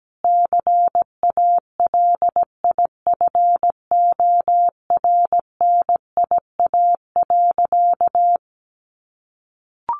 Morse code, heard as a keyed tone, about 17 wpm, 700 Hz. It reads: CALIFORNIA.
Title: Easy Phrases 17wpm